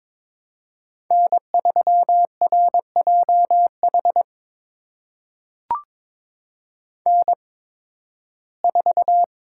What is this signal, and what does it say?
Morse code, heard as a keyed tone, about 22 wpm, spaced as for 14 wpm, 700 Hz. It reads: N3RJH N4
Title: USA Call Signs - 5 Characters - 8x Character Spacing - 22wpm